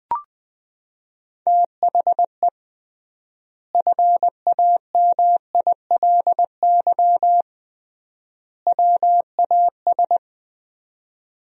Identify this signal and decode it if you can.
Morse code, heard as a keyed tone, about 21 wpm, 700 Hz. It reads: THE FAMILY WAS
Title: 3x Word-Spacing - Sentences from Top 400 Words - 20wpm